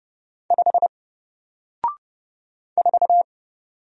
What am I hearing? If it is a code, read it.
Morse code, heard as a keyed tone, about 31 wpm, 700 Hz. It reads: 5 4
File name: Single-Letter-Number-30wpm